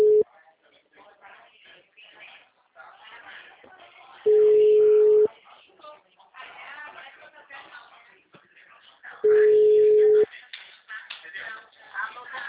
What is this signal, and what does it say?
Ringback tone (Europe, 425 Hz), with speech.
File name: out- -2023- - - .12